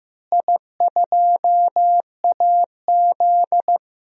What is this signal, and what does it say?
Morse code, heard as a keyed tone, about 15 wpm, 700 Hz. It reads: I2AZ
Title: DXCC-100 Call Signs - 4 Characters - Repeated 2x - 15wpm